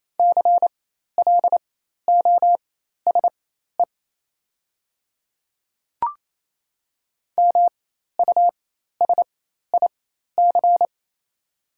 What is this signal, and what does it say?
Morse code, heard as a keyed tone, about 28 wpm, spaced as for 13 wpm, 700 Hz. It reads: CLOSE MUSIC
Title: 4x Character Spacing - Top 500 Words - 28wpm